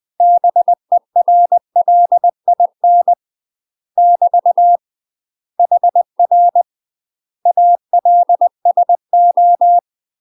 Morse code, heard as a keyed tone, about 20 wpm, 700 Hz. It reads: BERLIN <BT> HR ALSO
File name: UK MASTER QSOs TRAINING / 20WPM / UK MASTER QSOs 0003